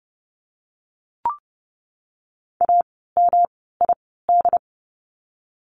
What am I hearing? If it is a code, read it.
Morse code, heard as a keyed tone, about 30 wpm, spaced as for 17 wpm, 700 Hz. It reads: AMID
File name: course / 30:10wpm / 108 Lesson 18f - Letter m - Words 30wpm